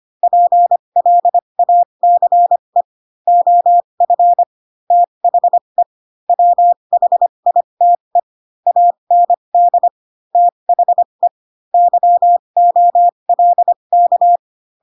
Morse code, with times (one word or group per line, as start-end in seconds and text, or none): 0.2-2.8 PLACE
3.3-4.4 OF
4.9-5.8 THE
6.3-8.2 WHITE
8.7-9.9 AND
10.3-11.3 THE
11.7-14.4 YOLK